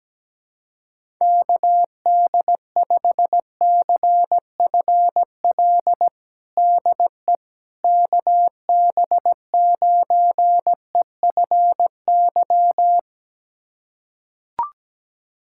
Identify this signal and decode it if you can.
Morse code, heard as a keyed tone, about 17 wpm, 700 Hz. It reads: KD5CFL DE KB9EFY
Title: USA Callsign DE Callsign 17wpm